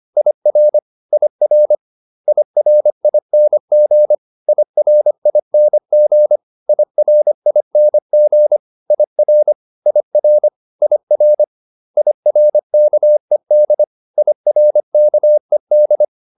Morse code, heard as a keyed tone, about 25 wpm, 600 Hz. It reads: IR IR IRING IRING IRING IR IR IR IRKED IRKED